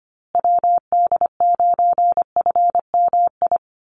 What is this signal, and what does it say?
Morse code, heard as a keyed tone, about 25 wpm, 700 Hz. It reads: WD9FMI
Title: USA Call Signs Repeated 3x 25wpm